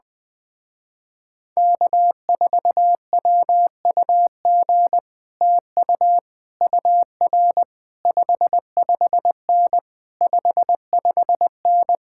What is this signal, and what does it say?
Morse code, heard as a keyed tone, about 21 wpm, 700 Hz. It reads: K4WUG TU UR 55N 55N
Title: Speed-Racing Common QSO Phrases - 20wpm